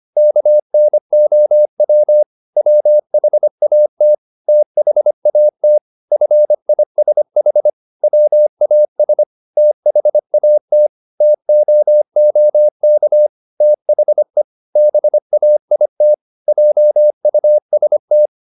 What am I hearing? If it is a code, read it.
Morse code, heard as a keyed tone, about 25 wpm, 600 Hz. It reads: KNOW WHAT THAT FISH WAS THAT TOOK THE BAIT JUST
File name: the old man and the sea part 1 25wpm